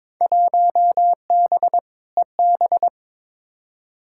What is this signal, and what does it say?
Morse code, heard as a keyed tone, about 22 wpm, 700 Hz. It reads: 1B EB
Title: Field Day Exchanges Encore - 22wpm